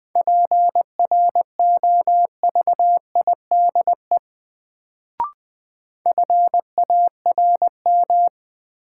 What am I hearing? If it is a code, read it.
Morse code, heard as a keyed tone, about 20 wpm, 700 Hz. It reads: PROVIDE FARM
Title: Sentences from Top 1000 Words - Review Part 2 - New Word Drill - 20wpm